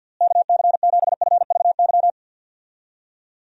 Morse code, heard as a keyed tone, about 49 wpm, 700 Hz. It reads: KC7FVY
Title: USA Call Signs - 6 Characters - Repeated 2x - 50wpm